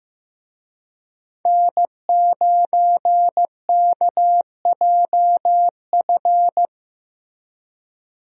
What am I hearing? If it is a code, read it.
Morse code, heard as a keyed tone, about 16 wpm, 700 Hz. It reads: N9KJF
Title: USA Call Signs - 5 Characters - Repeated 3x - 15wpm